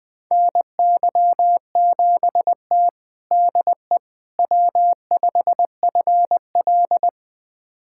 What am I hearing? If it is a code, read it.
Morse code, heard as a keyed tone, about 21 wpm, 700 Hz. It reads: NY7T DE W5FL
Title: USA Callsign DE Callsign - 4 Characters - 20wpm